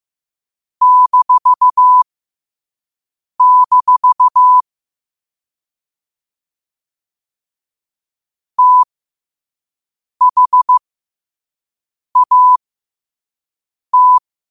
Morse code, heard as a keyed tone, about 15 wpm, spaced as for 5 wpm, 1000 Hz. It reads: -- THAT
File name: qotdfiles / 5wpm